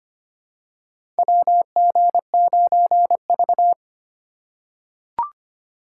Morse code, heard as a keyed tone, about 25 wpm, 690 Hz. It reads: WG9V
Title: USA Call Signs - 4 Characters - 4x Character Spacing - 25wpm